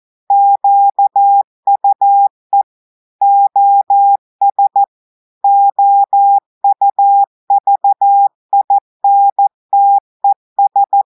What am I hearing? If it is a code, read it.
Morse code, heard as a keyed tone, about 14 wpm, 800 Hz. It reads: QUE OS OUVINTES
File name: audios / portugais / portugais texte en clair 2 vitesse 14 mots minutes